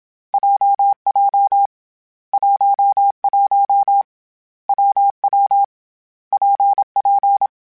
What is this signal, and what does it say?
Morse code, heard as a keyed tone, about 27 wpm, 800 Hz. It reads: JJ 11 WW PP
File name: Variable Speed / Variable Speed-mp3s / WPJ WPJ1 sp @25